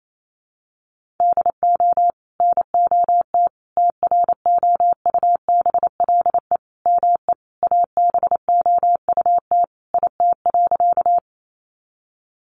Morse code, heard as a keyed tone, about 28 wpm, 700 Hz. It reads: DO NOT TROUBLE ME ABOUT IT.
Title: Sentences from Top 1200 Words 28wpm